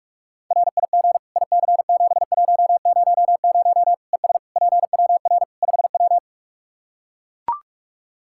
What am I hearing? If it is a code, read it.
Morse code, heard as a keyed tone, about 42 wpm, 700 Hz. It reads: RIG IC7100 ES PWR 5W
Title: Practice QSO - Exchange 2 - Info - 45wpm